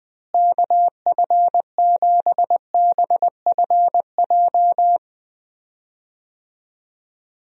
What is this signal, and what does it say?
Morse code, heard as a keyed tone, about 20 wpm, 700 Hz. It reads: KF7BFJ
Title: USA Call Signs - 6 Characters - Repeated 2x - 20wpm